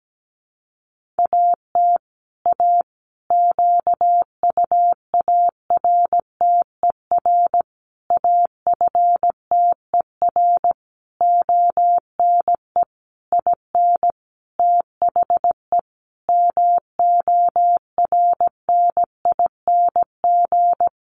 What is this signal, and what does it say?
Morse code, heard as a keyed tone, about 17 wpm, 700 Hz. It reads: AT A QUARTER AFTER ONE IN THE MORNING
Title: Sentences from Top 1300 Words 17wpm